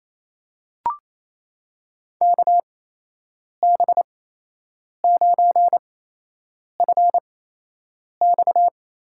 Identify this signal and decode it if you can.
Morse code, heard as a keyed tone, about 28 wpm, spaced as for 8 wpm, 700 Hz. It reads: KB9FX
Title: USA Call Signs - 6 Characters - 8x Character Spacing - 28wpm